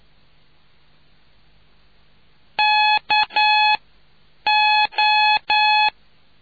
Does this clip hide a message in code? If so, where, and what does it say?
Yes: Morse code from 2.6 to 5.9 s, heard as a keyed tone, about 9 wpm, 850 Hz. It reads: K O